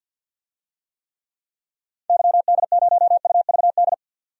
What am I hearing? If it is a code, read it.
Morse code, heard as a keyed tone, about 50 wpm, 700 Hz. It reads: YD0UVD
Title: DXCC-100 Call Signs - 6 Characters - 50wpm